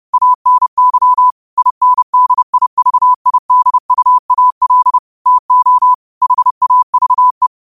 Morse code, heard as a keyed tone, about 30 wpm, 1000 Hz. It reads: ANY INDIVIDUAL TO HAVE
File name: qotdfiles / 30wpm